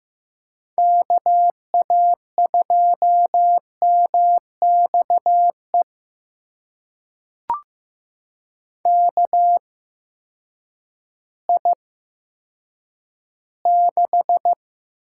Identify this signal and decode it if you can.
Morse code, heard as a keyed tone, about 15 wpm, spaced as for 9 wpm, 700 Hz. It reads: KA2MXE KI6
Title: USA Call Signs - 6 Characters - 8x Character Spacing - 15wpm